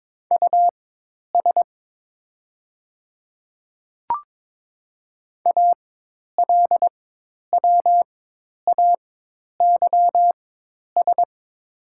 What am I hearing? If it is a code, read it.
Morse code, heard as a keyed tone, about 22 wpm, spaced as for 10 wpm, 700 Hz. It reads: US ALWAYS
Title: 4x Character Spacing - Top 300 Words - 22wpm